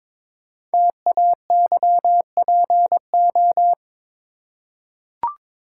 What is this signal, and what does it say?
Morse code, heard as a keyed tone, about 23 wpm, 700 Hz. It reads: TAYPO
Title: Code Groups - 5 Letters - 22wpm